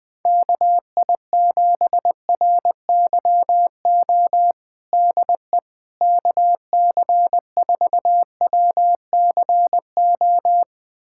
Morse code, heard as a keyed tone, about 20 wpm, 700 Hz. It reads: KI7RYO DE KC4WCO
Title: USA Callsign DE Callsign - 6 Characters - 20wpm